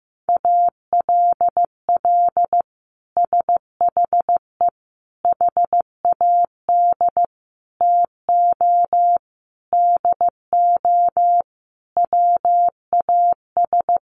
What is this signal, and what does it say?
Morse code, heard as a keyed tone, about 15 wpm, 700 Hz. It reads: ALL SHE HAD TO DO WAS
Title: Sentences from Top 1100 Words 15wpm